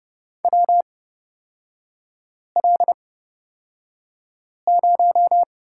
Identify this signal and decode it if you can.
Morse code, heard as a keyed tone, about 30 wpm, spaced as for 5 wpm, 700 Hz. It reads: WL0
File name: Lesson 36e - BT - Three Characters 30wpm